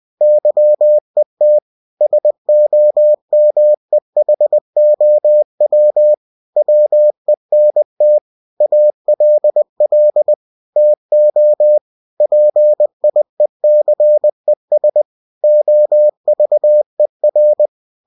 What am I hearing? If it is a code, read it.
Morse code, heard as a keyed tone, about 20 wpm, 600 Hz. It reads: YET SOMEHOW WENT ALL TO PIECES OVER
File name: cw-books / winnie-the-pooh / winnie-20wpm-page-0029